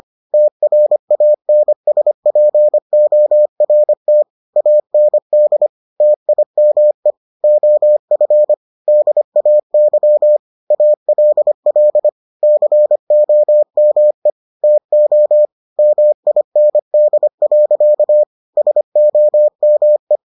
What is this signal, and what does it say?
Morse code, heard as a keyed tone, about 25 wpm, 600 Hz. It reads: TRANSPORT AND TIME OF DAY ALL COME TO MIND. SOME